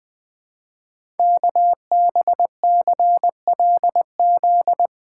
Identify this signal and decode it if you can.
Morse code, heard as a keyed tone, about 20 wpm, 700 Hz. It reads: KBCLZ
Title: Code Groups - 5 Letters - 20wpm